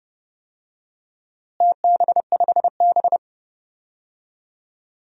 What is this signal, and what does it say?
Morse code, heard as a keyed tone, about 30 wpm, 700 Hz. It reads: TB5B
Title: DXCC-100 Call Signs - 4 Characters - Repeated 2x - 30wpm